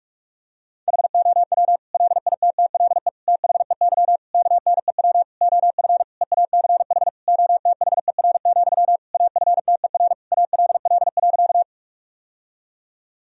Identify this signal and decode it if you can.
Morse code, heard as a keyed tone, about 46 wpm, 700 Hz. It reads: HOW LITTLE THEY KNEW OF EACH OTHER, AFTER ALL.
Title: Sentences from Top 1000 Words - Review Part 1 - 45wpm